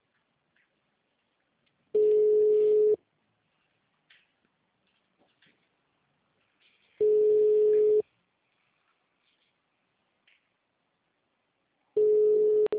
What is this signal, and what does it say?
Ringback tone (Europe, 425 Hz).
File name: out- -2009- - - .9629